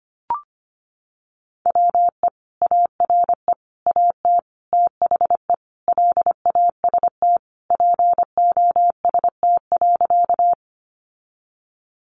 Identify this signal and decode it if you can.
Morse code, heard as a keyed tone, about 25 wpm, 700 Hz. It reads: WE ARE AT THE LAST POST.
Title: Sentences from Top 1000 Words 25wpm